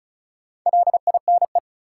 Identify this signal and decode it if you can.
Morse code, heard as a keyed tone, about 36 wpm, 700 Hz. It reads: LINE
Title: Sentences from Top 1000 Words - Review Part 1 - New Word Drill - 35wpm